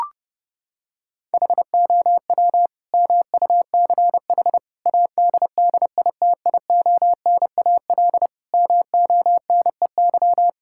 Morse code, heard as a keyed tone, about 30 wpm, 700 Hz. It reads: HOW MUCH ADDITIONAL MONEY
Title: Sentences from Top 1000 Words - Review Part 5 - 30wpm